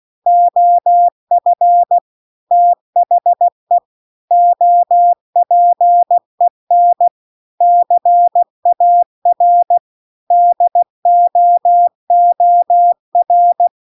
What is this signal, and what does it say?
Morse code, heard as a keyed tone, about 16 wpm, 700 Hz. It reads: OF THE OPEN CAR DOOR